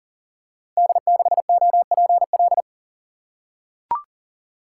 Morse code, heard as a keyed tone, about 40 wpm, 690 Hz. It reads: D6OPL
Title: Code Groups - 5 Letter:Numbers - 40wpm